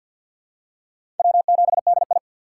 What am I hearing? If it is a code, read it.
Morse code, heard as a keyed tone, about 52 wpm, 690 Hz. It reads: W7DI
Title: USA Call Signs - 3x Character Spacing - 50wpm